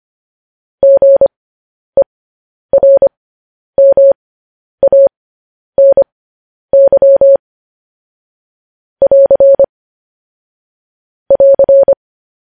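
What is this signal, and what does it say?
Morse code, heard as a keyed tone, about 25 wpm, spaced as for 10 wpm, 550 Hz. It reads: GERMANY <AR> <AR>
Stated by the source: Morse code news headlines sent at 10 wpm.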